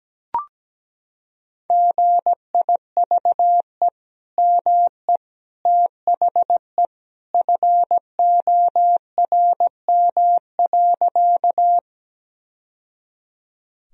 Morse code, heard as a keyed tone, about 17 wpm, 700 Hz. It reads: GIVE ME THE FORM.
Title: Sentences from Top 1000 Words - Review Part 1 - 17wpm